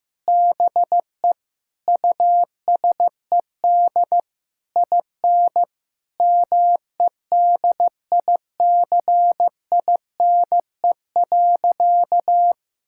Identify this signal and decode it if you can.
Morse code, heard as a keyed tone, about 16 wpm, 700 Hz. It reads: BE USED IN MEDICINE.
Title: Sentences from Top 1000 Words - Review Part 6 - 15wpm